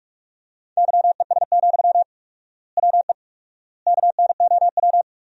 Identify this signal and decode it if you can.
Morse code, heard as a keyed tone, about 45 wpm, 700 Hz. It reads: YES, WE KNOW
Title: 4x Word-Spacing - Sentences from Top 500 Words - 45wpm